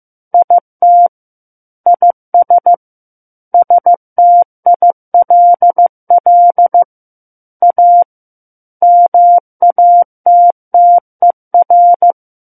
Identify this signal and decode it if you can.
Morse code, heard as a keyed tone, about 15 wpm, 700 Hz. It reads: IT IS STILL A MATTER
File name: Book One / 06-Chapter 6 - 15 WPM